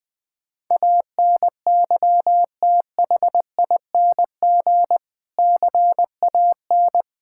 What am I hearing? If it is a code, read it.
Morse code, heard as a keyed tone, about 21 wpm, 700 Hz. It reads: ANYTHING CAN
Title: Sentences from Top 1000 Words - Review Part 5 - 20wpm